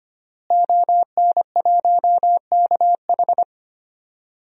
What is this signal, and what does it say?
Morse code, heard as a keyed tone, about 25 wpm, 700 Hz. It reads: ON1KH
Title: DXCC-100 Call Signs - 5 Characters - Repeated 3x - 25wpm